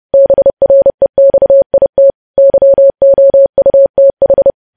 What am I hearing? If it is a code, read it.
Morse code, heard as a keyed tone, about 30 wpm, 550 Hz. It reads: BREXIT YOUTH
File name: podcast / News headlines-1x-30x30